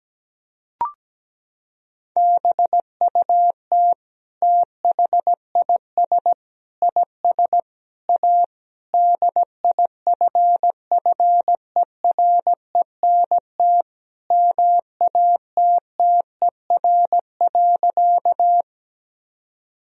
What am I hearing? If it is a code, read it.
Morse code, heard as a keyed tone, about 17 wpm, 700 Hz. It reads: BUT THIS IS A DIFFERENT MATTER.
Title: Sentences from Top 1000 Words - Review Part 3 - 17wpm